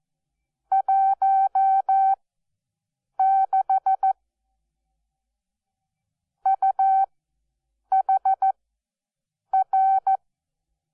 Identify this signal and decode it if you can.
Morse code, heard as a keyed tone, about 14 wpm, spaced as for 7 wpm, 770 Hz. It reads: 16 UHR
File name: Onlinelehrgang / morsen / DJ4UF Morsen12